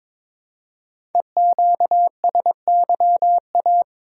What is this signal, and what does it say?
Morse code, heard as a keyed tone, about 23 wpm, 690 Hz. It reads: EQSYA
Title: Code Groups - 5 Letters - 22wpm